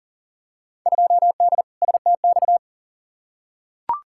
Morse code, heard as a keyed tone, about 40 wpm, 700 Hz. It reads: 2D STX
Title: Field Day Exchanges - 40wpm